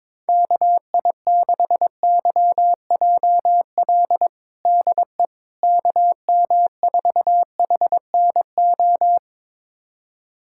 Morse code, heard as a keyed tone, about 22 wpm, 700 Hz. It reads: KI6YJL DE KM4HNO
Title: USA Callsign DE Callsign - 6 Characters - 22wpm